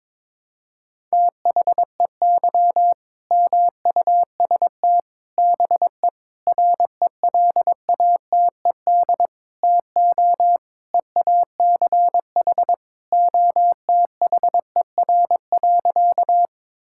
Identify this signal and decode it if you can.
Morse code, heard as a keyed tone, about 22 wpm, 700 Hz. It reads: THEY MUST BE RELATED TO EACH OTHER.
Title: Sentences from Top 1000 Words - Review Part 5 - 22wpm